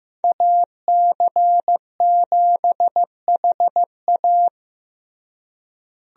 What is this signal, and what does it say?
Morse code, heard as a keyed tone, about 15 wpm, 700 Hz. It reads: AC7HA
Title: USA Call Signs - 5 Characters - Repeated 2x - 15wpm